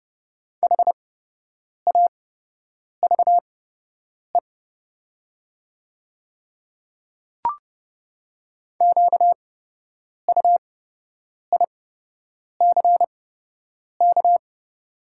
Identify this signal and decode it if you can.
Morse code, heard as a keyed tone, about 30 wpm, spaced as for 8 wpm, 690 Hz. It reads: HAVE QUICK
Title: 8x Character Spacing - Top 500 Words - 30wpm